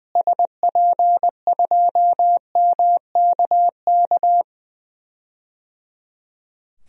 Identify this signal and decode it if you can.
Morse code, heard as a keyed tone, about 20 wpm, 700 Hz. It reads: SP2MKK
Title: DXCC-100 Call Signs - 6 Characters - Repeated 3x - 20wpm